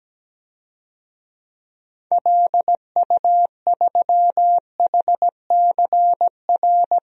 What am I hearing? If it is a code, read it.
Morse code, heard as a keyed tone, about 17 wpm, 700 Hz. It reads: LU3HCR
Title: DXCC-100 Call Signs - 6 Characters - 17wpm